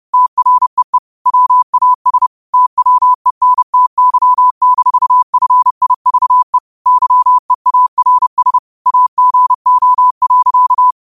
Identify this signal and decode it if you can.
Morse code, heard as a keyed tone, about 30 wpm, 1000 Hz. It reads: TREE WAS TWENTY-FIVE YEARS AGO.
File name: qotdfiles / 30wpm